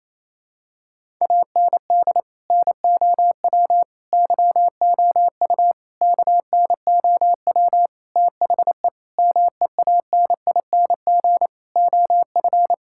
Morse code, heard as a keyed tone, about 28 wpm, 700 Hz. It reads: AND NOW YOU KNOW THE MEANING OF
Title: Sentences from Top 1000 Words - Review Part 6 - 28wpm